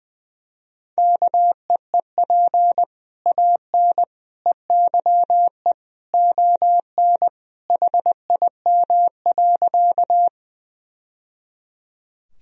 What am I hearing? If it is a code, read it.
Morse code, heard as a keyed tone, about 20 wpm, 700 Hz. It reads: KEEP AN EYE ON HIM.
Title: Sentences from Top 1000 Words - Review Part 1 - 20wpm